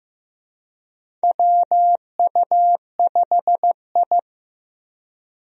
Morse code, heard as a keyed tone, about 15 wpm, 700 Hz. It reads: WU5I
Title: USA Call Signs - 4 Characters - Repeated 2x - 15wpm